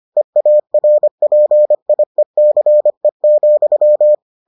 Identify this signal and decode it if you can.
Morse code, heard as a keyed tone, about 25 wpm, 600 Hz. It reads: EARPIECE,